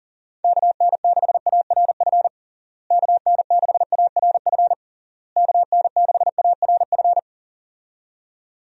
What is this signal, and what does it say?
Morse code, heard as a keyed tone, about 40 wpm, 700 Hz. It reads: KN6ARF KN6ARF KN6ARF
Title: USA Call Signs - 6 Characters - Repeated 3x - 40wpm